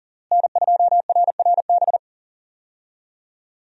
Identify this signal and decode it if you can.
Morse code, heard as a keyed tone, about 40 wpm, 700 Hz. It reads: N2RRB
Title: USA Call Signs - 5 Characters - Repeated 3x - 40wpm